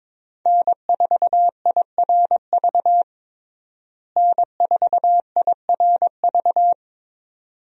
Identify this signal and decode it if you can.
Morse code, heard as a keyed tone, about 22 wpm, 700 Hz. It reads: N4IRV N4IRV
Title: USA Call Signs - 5 Characters - Repeated 2x - 22wpm